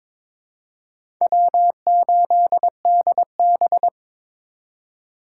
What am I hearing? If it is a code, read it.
Morse code, heard as a keyed tone, about 22 wpm, 700 Hz. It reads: W8DB
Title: USA Call Signs - 4 Characters - 2x Character Spacing - 22wpm